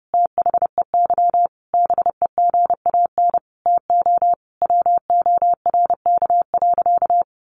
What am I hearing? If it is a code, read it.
Morse code, heard as a keyed tone, about 30 wpm, 700 Hz. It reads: THEY BEGAN TO WORK.
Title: Sentences from Top 1400 Words 30wpm